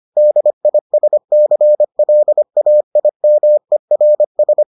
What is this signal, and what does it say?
Morse code, heard as a keyed tone, about 25 wpm, 600 Hz. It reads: DISCLAIMERS